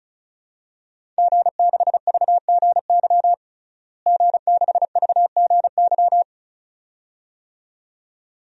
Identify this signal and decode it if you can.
Morse code, heard as a keyed tone, about 34 wpm, 700 Hz. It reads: G6VGY G6VGY
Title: DXCC-100 Call Signs - 5 Characters - Repeated 2x - 35wpm